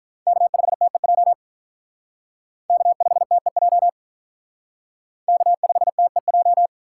Morse code, heard as a keyed tone, about 43 wpm, 700 Hz. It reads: K5TEJ K5TEJ K5TEJ
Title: Speed Racing - USA 1x3 Call Signs 35wpm